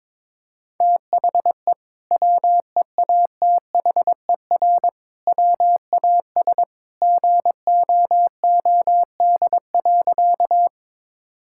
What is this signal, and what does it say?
Morse code, heard as a keyed tone, about 22 wpm, 700 Hz. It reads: THE WEATHER WAS GOOD.
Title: Sentences from Top 1000 Words - Review Part 5 - 22wpm